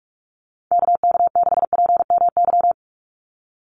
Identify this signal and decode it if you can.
Morse code, heard as a keyed tone, about 44 wpm, 700 Hz. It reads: KK6PMY
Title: USA Call Signs 45wpm